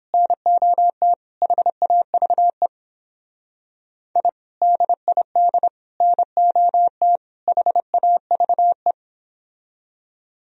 Morse code, heard as a keyed tone, about 29 wpm, 700 Hz. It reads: NOT HAVE I DID NOT HAVE